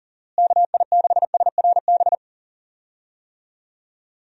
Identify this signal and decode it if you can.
Morse code, heard as a keyed tone, about 40 wpm, 700 Hz. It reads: KI6SRB
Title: USA Call Signs - 6 Characters - Repeated 3x - 40wpm